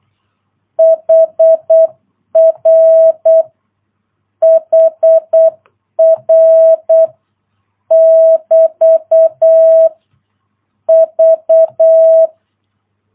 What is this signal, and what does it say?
Morse code, heard as a keyed tone, about 8 wpm, 650 Hz. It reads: HR HR <BT> V